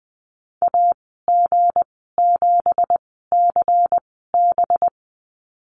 Morse code, heard as a keyed tone, about 20 wpm, spaced as for 14 wpm, 700 Hz. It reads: AG7CB
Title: 213 Lesson 34g - Number 8 - Call Signs 20wpm